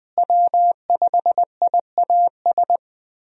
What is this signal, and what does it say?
Morse code, heard as a keyed tone, about 20 wpm, 700 Hz. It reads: W5IAS
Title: USA Call Signs - 5 Characters - Repeated 3x - 20wpm